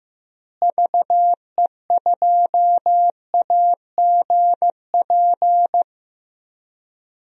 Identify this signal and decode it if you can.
Morse code, heard as a keyed tone, about 15 wpm, 700 Hz. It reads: VE2AGP
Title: DXCC-100 Call Signs - 6 Characters - Repeated 2x - 15wpm